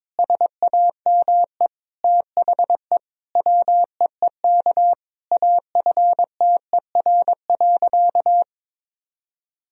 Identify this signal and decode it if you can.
Morse code, heard as a keyed tone, about 22 wpm, 700 Hz. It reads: SAME THE WEEK AFTER.
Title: Sentences from Top 500 Words and Abbreviations 22wpm